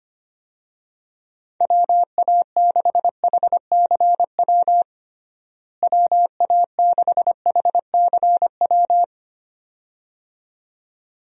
Morse code, heard as a keyed tone, about 25 wpm, 700 Hz. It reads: WA6HCW WA6HCW
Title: USA Call Signs - 6 Characters - Repeated 2x - 25wpm